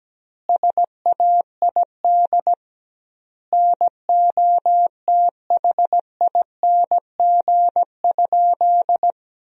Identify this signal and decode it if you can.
Morse code, heard as a keyed tone, about 17 wpm, 700 Hz. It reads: SAID NOTHING?
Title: 2x Word-Spacing - Sentences from Top 500 Words - 17wpm